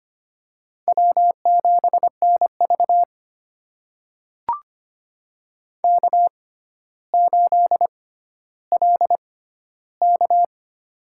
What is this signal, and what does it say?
Morse code, heard as a keyed tone, about 25 wpm, spaced as for 13 wpm, 700 Hz. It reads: W7NV K8LK
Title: USA Call Signs - 6x Character Spacing - 25wpm